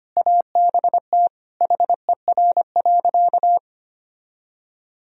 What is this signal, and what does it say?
Morse code, heard as a keyed tone, about 25 wpm, 700 Hz. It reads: ABT HER.
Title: Sentences from Top 200 Words and Abbreviations 25wpm